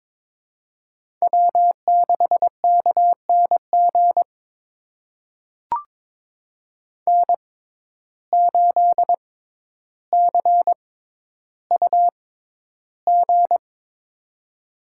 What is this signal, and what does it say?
Morse code, heard as a keyed tone, about 22 wpm, spaced as for 11 wpm, 700 Hz. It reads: W6KNG N8CUG
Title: USA Call Signs - 5 Characters - 6x Character Spacing - 22wpm